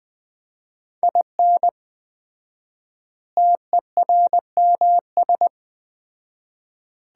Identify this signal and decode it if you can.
Morse code, heard as a keyed tone, about 20 wpm, 700 Hz. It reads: IN TERMS
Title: 4x Word-Spacing - Sets of 4 Words - 20wpm